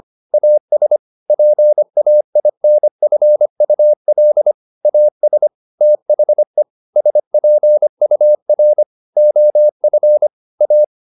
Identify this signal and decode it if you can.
Morse code, heard as a keyed tone, about 25 wpm, 600 Hz. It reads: AS PAINFUL AS THE SPUR OF A
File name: the old man and the sea part 1 25wpm